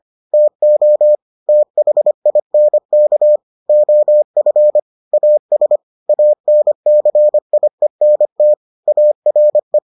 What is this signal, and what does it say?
Morse code, heard as a keyed tone, about 25 wpm, 600 Hz. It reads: TO THINK OF AS ANCIENT ARE